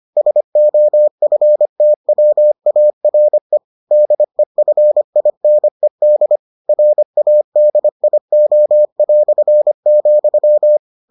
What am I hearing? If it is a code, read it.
Morse code, heard as a keyed tone, about 25 wpm, 600 Hz. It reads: SOFTWARE DEFINED RADIO",